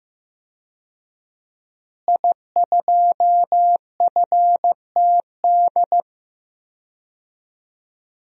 Morse code, heard as a keyed tone, about 16 wpm, 700 Hz. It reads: I2FTD
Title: DXCC-100 Call Signs - 5 Characters - Repeated 3x - 15wpm